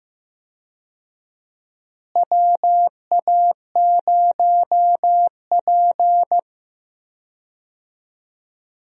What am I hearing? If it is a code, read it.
Morse code, heard as a keyed tone, about 15 wpm, 700 Hz. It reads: WA0P
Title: USA Call Signs - 4 Characters - Repeated 2x - 15wpm